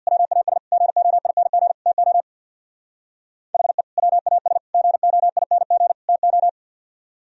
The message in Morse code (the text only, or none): WAS GOING TO HE WAS GOING TO